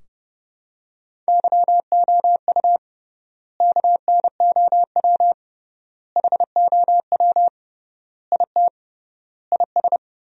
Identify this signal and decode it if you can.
Morse code, heard as a keyed tone, about 32 wpm, 700 Hz. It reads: YOU KNOW HOW IT IS
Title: 3x Word-Spacing - Sentences from Top 100 Words - 30wpm